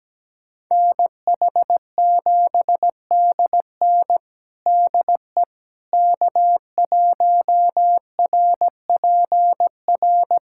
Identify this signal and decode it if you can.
Morse code, heard as a keyed tone, about 17 wpm, 700 Hz. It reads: NH7DN DE K1RPR
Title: USA Callsign DE Callsign 17wpm